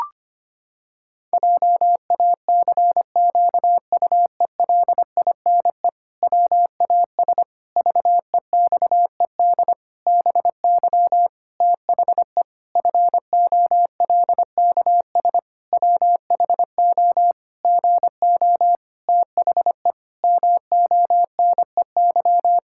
Morse code, heard as a keyed tone, about 25 wpm, 700 Hz. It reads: JACQUELINE WAS VEXED BY THE FOLKS WHO GO THE MONEY
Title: Pangram Challenge v2 - 25wpm